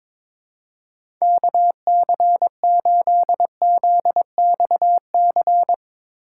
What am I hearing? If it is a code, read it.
Morse code, heard as a keyed tone, about 22 wpm, 700 Hz. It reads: KC8ZXC
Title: USA Call Signs - 6 Characters - 3x Character Spacing - 22wpm